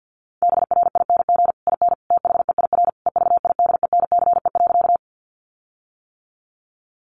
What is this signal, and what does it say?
Morse code, heard as a keyed tone, about 50 wpm, 700 Hz. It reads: BRING IN THEIR EVIDENCE.
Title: Sentences from Top 1300 Words 50wpm